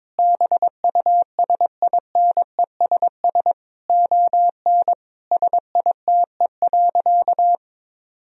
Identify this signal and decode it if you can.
Morse code, heard as a keyed tone, about 22 wpm, 700 Hz. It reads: BUSINESS ON SITE.
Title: Sentences from Top 1000 Words - Review Part 4 - 22wpm